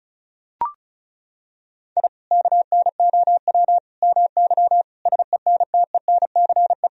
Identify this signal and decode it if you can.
Morse code, heard as a keyed tone, about 36 wpm, 700 Hz. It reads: I KNOW MY SENTENCE
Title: Sentences from Top 1000 Words - Review Part 1 - 35wpm